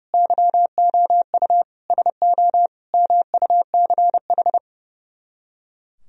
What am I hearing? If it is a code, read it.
Morse code, heard as a keyed tone, about 37 wpm, 700 Hz. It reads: YOU SO MUCH
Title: Speed-Racing - Sets of 4 Words 30wpm